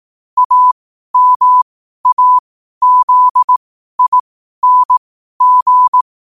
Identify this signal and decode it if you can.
Morse code, heard as a keyed tone, about 18 wpm, spaced as for 12 wpm, 1000 Hz. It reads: AMAZING
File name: qotdfiles / 12wpm